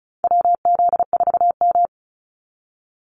Morse code, heard as a keyed tone, about 34 wpm, 700 Hz. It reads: WZ4M
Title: NA SOTA Activator Callsigns 35wpm